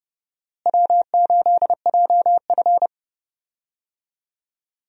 Morse code, heard as a keyed tone, about 30 wpm, 700 Hz. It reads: W8JF
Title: USA Call Signs - 4 Characters - Repeated 2x - 30wpm